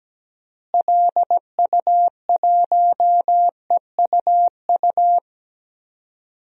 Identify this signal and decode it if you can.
Morse code, heard as a keyed tone, about 17 wpm, 700 Hz. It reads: LU1EUU
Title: DXCC-100 Call Signs - 6 Characters - Repeated 2x - 17wpm